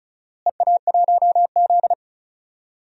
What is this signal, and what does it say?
Morse code, heard as a keyed tone, about 34 wpm, 700 Hz. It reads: EA1Z
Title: DXCC-100 Call Signs - 4 Characters - Repeated 3x - 35wpm